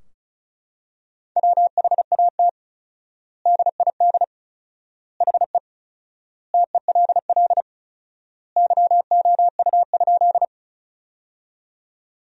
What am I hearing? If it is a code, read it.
Morse code, heard as a keyed tone, about 36 wpm, 700 Hz. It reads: WHAT DID HE TELL YOU?
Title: 4x Word-Spacing - Sentences from Top 200 Words - 35wpm